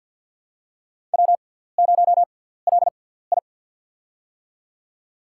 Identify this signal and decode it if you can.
Morse code, heard as a keyed tone, about 54 wpm, spaced as for 17 wpm, 700 Hz. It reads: W0LI
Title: USA Call Signs - 4 Characters - 6x Character Spacing - 50wpm